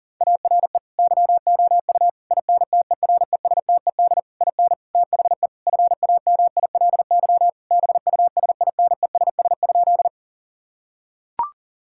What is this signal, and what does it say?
Morse code, heard as a keyed tone, about 37 wpm, 700 Hz. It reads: ARE YOU INTERESTED IN THE FAMILY BUSINESS?
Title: Sentences from Top 1000 Words - Review Part 6 - 40wpm